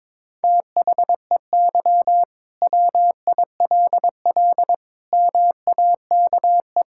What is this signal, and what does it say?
Morse code, heard as a keyed tone, about 22 wpm, 700 Hz. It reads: THEY WILL MAKE
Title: Sentences from Top 1600 Words 22wpm